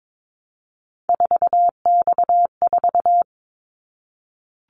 Morse code, heard as a keyed tone, about 22 wpm, 700 Hz. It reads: 4X4
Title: Callsign Prefixes for Top 50 DXCC - 22wpm